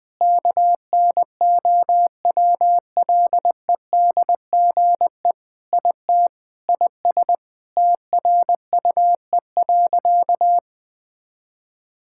Morse code, heard as a keyed tone, about 20 wpm, 700 Hz. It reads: KNOWLEDGE IT IS TRUE.
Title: Sentences with 6-Word Phrases - 20wpm